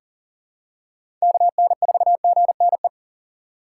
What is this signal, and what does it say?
Morse code, heard as a keyed tone, about 40 wpm, 700 Hz. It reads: KN4GNE
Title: USA Call Signs - 3x Character Spacing - 40wpm